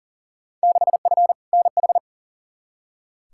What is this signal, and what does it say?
Morse code, heard as a keyed tone, about 40 wpm, 700 Hz. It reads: 6F NH
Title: Field Day Exchanges Encore - 40wpm